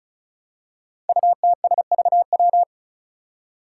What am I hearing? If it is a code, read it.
Morse code, heard as a keyed tone, about 36 wpm, 700 Hz. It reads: UTSVW
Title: Five Challenging Letter-Number Combinations 35wpm